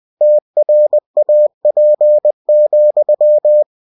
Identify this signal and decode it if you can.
Morse code, heard as a keyed tone, about 20 wpm, 600 Hz. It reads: TRAP,
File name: cw-books / winnie-the-pooh / winnie-20wpm-page-0036